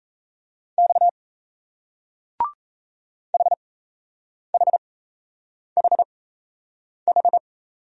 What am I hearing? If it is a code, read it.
Morse code, heard as a keyed tone, about 33 wpm, spaced as for 8 wpm, 700 Hz. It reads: X HHHH
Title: Speed-Racing Single Letter-Number 28wpm